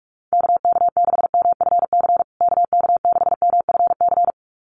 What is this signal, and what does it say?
Morse code, heard as a keyed tone, about 44 wpm, 700 Hz. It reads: KK6MFC KK6MFC
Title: USA Call Signs Repeated 2x 45wpm